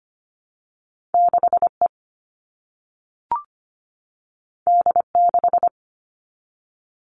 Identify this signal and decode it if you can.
Morse code, heard as a keyed tone, about 25 wpm, 700 Hz. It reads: 6E D6
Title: All Cut Numbers - 2-Digit 25wpm